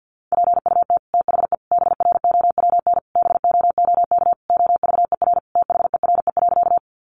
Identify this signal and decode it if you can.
Morse code, heard as a keyed tone, about 50 wpm, 700 Hz. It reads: PUT THE BROWN BOOK OVER THERE.
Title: Sentences from Top 1200 Words 50wpm